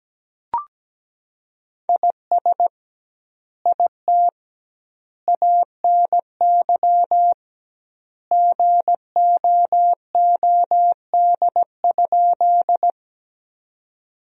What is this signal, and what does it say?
Morse code, heard as a keyed tone, about 17 wpm, 700 Hz. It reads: IS IT ANY GOOD?
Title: 2x Word-Spacing - Sentences from Top 200 Words - 17wpm